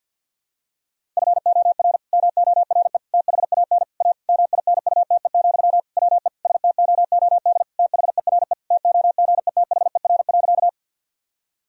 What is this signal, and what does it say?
Morse code, heard as a keyed tone, about 46 wpm, 700 Hz. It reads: FOR MORE THAN A MINUTE, WE STOOD THERE TOGETHER.
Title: Sentences from Top 1000 Words - Review Part 2 - 50wpm